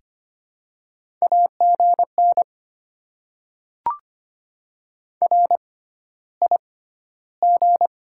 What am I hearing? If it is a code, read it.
Morse code, heard as a keyed tone, about 25 wpm, spaced as for 13 wpm, 690 Hz. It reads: AGN RIG
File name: Top 20 QSO Elements - 6x Character Spacing - 25wpm